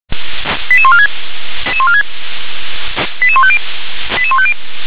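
Five-tone selective call (ZVEI), heard as quick runs of five tones, tones of about 70 ms. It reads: 88146 88150 88150
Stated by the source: ZVEI five-tone selective call 88913, received on SDR.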